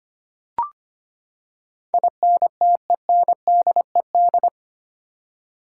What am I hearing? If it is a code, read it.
Morse code, heard as a keyed tone, about 25 wpm, 700 Hz. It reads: INTENDED
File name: Eight Letter Words - 25wpm